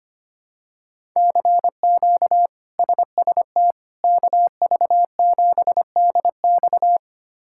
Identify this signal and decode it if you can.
Morse code, heard as a keyed tone, about 25 wpm, 700 Hz. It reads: CQ SST KV7DX
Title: K1USN SST QSO Practice - 25wpm